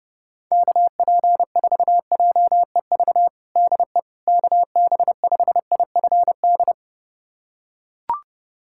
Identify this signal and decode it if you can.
Morse code, heard as a keyed tone, about 30 wpm, 700 Hz. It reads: KP4JEV DE KB5IFD
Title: USA Callsign DE Callsign 30wpm